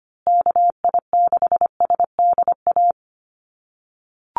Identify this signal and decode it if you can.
Morse code, heard as a keyed tone, about 25 wpm, 700 Hz. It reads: KI6SDA
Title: USA Call Signs Repeated 3x 25wpm